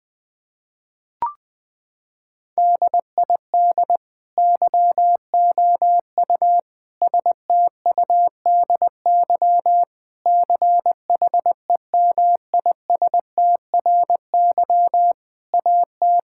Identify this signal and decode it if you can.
Morse code, heard as a keyed tone, about 21 wpm, 700 Hz. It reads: DID YOU STUDY CHEMISTRY AT
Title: Sentences from Top 1000 Words - Review Part 6 - 20wpm